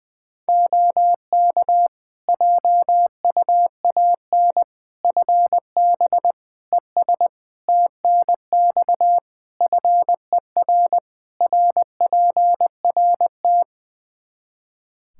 Morse code, heard as a keyed tone, about 20 wpm, 700 Hz. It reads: OK JUAN FB ES TNX FER RPRT
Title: K1USN SST QSO Practice 20wpm